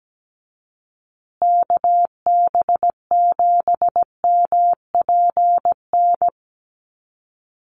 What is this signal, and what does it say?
Morse code, heard as a keyed tone, about 17 wpm, 700 Hz. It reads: KB7MPN
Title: USA Call Signs Repeated 3x 17wpm